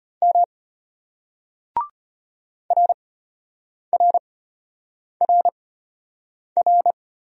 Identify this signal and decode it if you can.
Morse code, heard as a keyed tone, about 31 wpm, spaced as for 8 wpm, 700 Hz. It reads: M RRRR
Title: Speed-Racing Single Letters 25wpm